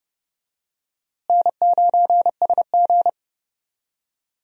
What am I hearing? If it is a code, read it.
Morse code, heard as a keyed tone, about 32 wpm, 700 Hz. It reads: N9SG
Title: USA Call Signs - 4 Characters - Repeated 2x - 30wpm